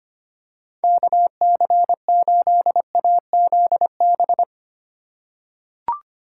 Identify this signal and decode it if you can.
Morse code, heard as a keyed tone, about 25 wpm, 700 Hz. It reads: KC8AZB
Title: USA Call Signs - 8x Character Spacing - 25wpm